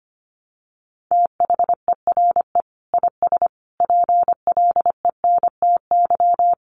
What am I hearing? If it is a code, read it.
Morse code, heard as a keyed tone, about 25 wpm, 700 Hz. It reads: THERE IS PLENTY
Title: Sentences from Top 1300 Words 25wpm